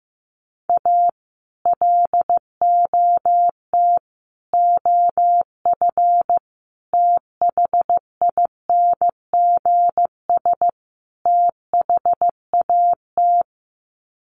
Morse code, heard as a keyed tone, about 15 wpm, 700 Hz. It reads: A LOT OF THINGS THAT
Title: Sets of 5 Words 15wpm